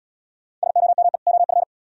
Morse code, heard as a keyed tone, about 78 wpm, 700 Hz. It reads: SOME OF